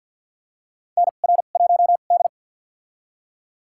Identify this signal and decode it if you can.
Morse code, heard as a keyed tone, about 50 wpm, spaced as for 36 wpm, 700 Hz. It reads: NR1D
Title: USA Call Signs - 4 Characters - 2x Character Spacing - 50wpm